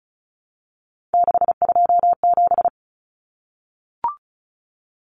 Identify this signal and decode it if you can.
Morse code, heard as a keyed tone, about 34 wpm, 700 Hz. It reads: 627
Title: Common Cut Numbers - 3-Digit 35wpm